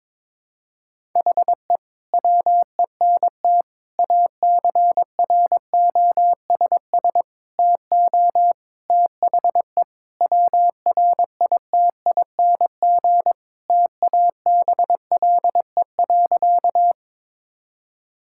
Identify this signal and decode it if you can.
Morse code, heard as a keyed tone, about 21 wpm, 700 Hz. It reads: HE WENT ACROSS TO THE WRITING TABLE.
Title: Sentences from Top 1000 Words - Review Part 5 - 22wpm